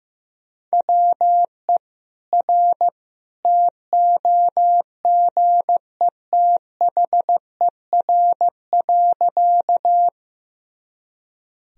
Morse code, heard as a keyed tone, about 16 wpm, 700 Hz. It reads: WE R TOGETHER.
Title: Sentences from Top 300 Words and Abbreviations 15wpm